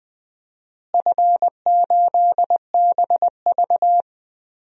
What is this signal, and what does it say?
Morse code, heard as a keyed tone, about 20 wpm, 700 Hz. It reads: F8BV
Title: DXCC-100 Call Signs - 4 Characters - Repeated 2x - 20wpm